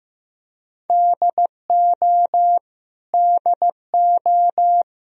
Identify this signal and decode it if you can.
Morse code, heard as a keyed tone, about 15 wpm, 700 Hz. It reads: DO DO
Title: Top 100 Words Repeated 15wpm